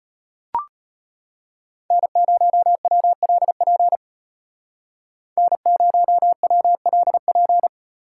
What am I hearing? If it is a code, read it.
Morse code, heard as a keyed tone, about 33 wpm, 700 Hz. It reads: N0WLP N0WLP